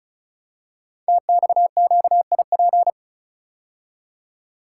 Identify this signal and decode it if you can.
Morse code, heard as a keyed tone, about 36 wpm, 700 Hz. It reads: TXQIP
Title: Code Groups - 5 Letters - 35wpm